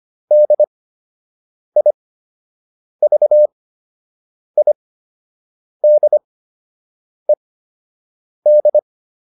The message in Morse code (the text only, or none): DIVIDED